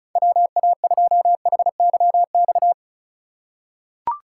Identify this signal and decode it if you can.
Morse code, heard as a keyed tone, about 34 wpm, 700 Hz. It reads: WA2HYX
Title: USA Call Signs - 6 Characters - 4x Character Spacing - 35wpm